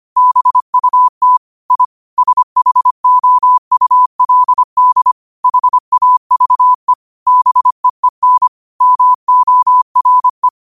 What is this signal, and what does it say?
Morse code, heard as a keyed tone, about 25 wpm, 1000 Hz. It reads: BUT I SHOULD HAVE BEEN MORE
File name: qotdfiles / 25wpm